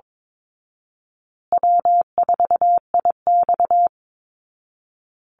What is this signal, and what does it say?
Morse code, heard as a keyed tone, about 22 wpm, 700 Hz. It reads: W4IX
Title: Call Signs Repeated 2x - 22wpm